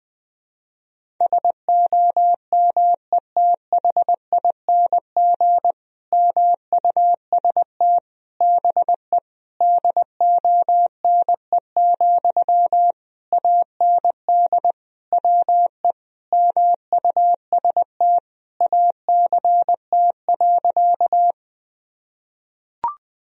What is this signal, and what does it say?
Morse code, heard as a keyed tone, about 21 wpm, 700 Hz. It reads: SOMETHING MUST BE DONE, AND WE MUST ACT.
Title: Sentences from Top 1000 Words - Review Part 1 - 20wpm